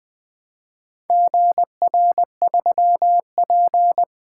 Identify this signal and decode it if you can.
Morse code, heard as a keyed tone, about 20 wpm, 700 Hz. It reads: GR3P
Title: Four Letter-Number Combinations - 20wpm